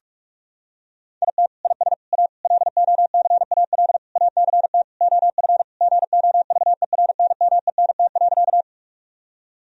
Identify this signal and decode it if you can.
Morse code, heard as a keyed tone, about 46 wpm, 700 Hz. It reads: IT IS A LOCAL ACT OF GOVERNMENT.
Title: Sentences from Top 1000 Words - Review Part 3 - 45wpm